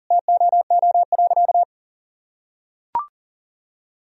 Morse code, heard as a keyed tone, about 40 wpm, 700 Hz. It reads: TOO.
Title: Sentences from Top 1000 Words - Review Part 3 - 40wpm